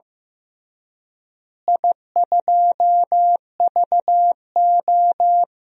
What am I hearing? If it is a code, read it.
Morse code, heard as a keyed tone, about 16 wpm, 700 Hz. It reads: I2VO
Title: DXCC-100 Call Signs - 4 Characters - Repeated 2x - 15wpm